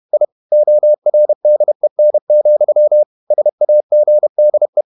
Morse code, heard as a keyed tone, about 31 wpm, 600 Hz. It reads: I ORDEN, SAGDE
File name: Ferie QRQ 31wpm